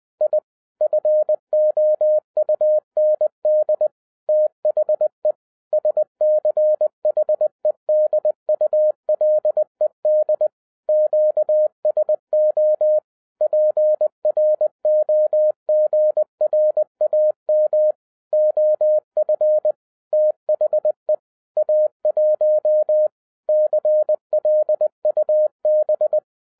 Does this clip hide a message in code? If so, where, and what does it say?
Yes: Morse code from 0.2 to 26.2 s, heard as a keyed tone, about 20 wpm, 610 Hz. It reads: I FOUND THE SCHEDULED QSO PROGRAM OF THE A1 CLUB